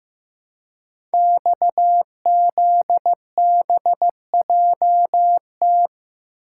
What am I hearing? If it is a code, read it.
Morse code, heard as a keyed tone, about 15 wpm, 700 Hz. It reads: XZBJT
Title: Code Groups - 5 Letter:Numbers - 15wpm